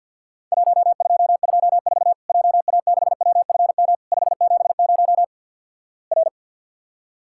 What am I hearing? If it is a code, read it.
Morse code, heard as a keyed tone, about 50 wpm, 700 Hz. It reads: 1224 JA6WFM 570 R
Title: Worked All Europe DX Contest - QTC Exchange with Repeats - 50wpm